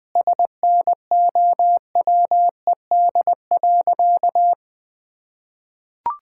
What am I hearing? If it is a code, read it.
Morse code, heard as a keyed tone, about 20 wpm, 700 Hz. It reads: SNOWED.
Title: Sentences with 5-Word Phrases - Spoken Beginning - 20wpm